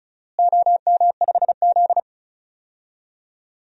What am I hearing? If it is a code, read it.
Morse code, heard as a keyed tone, about 34 wpm, 700 Hz. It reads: OM5Z
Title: DXCC-100 Call Signs - 4 Characters - Repeated 3x - 35wpm